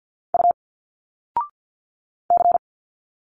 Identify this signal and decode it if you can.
Morse code, heard as a keyed tone, about 49 wpm, 700 Hz. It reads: U C
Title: Single Letters 50wpm